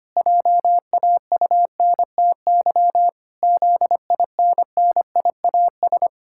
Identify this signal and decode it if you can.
Morse code, heard as a keyed tone, about 25 wpm, 700 Hz. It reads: JAUNTY ZINNIAS
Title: Pangram Challenge v2 - 25wpm